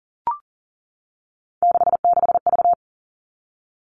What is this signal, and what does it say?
Morse code, heard as a keyed tone, about 40 wpm, 700 Hz. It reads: 66V
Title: Three Challenging Letter-Number Combinations 40wpm